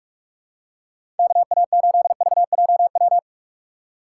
Morse code, heard as a keyed tone, about 42 wpm, 700 Hz. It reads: KA8VJW
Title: USA Call Signs - 6 Characters - 4x Character Spacing - 45wpm